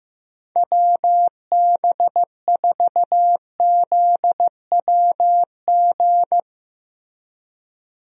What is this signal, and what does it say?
Morse code, heard as a keyed tone, about 15 wpm, 700 Hz. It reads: WB4ZWG
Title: USA Call Signs - 6 Characters - Repeated 3x - 15wpm